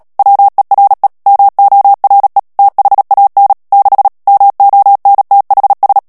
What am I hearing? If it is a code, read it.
Morse code, heard as a keyed tone, about 37 wpm, 800 Hz. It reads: WERE MORE THAN 6 MONTHS
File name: morse mp3 / ANGLAIS texte clair 3 vitesse 35 mots minute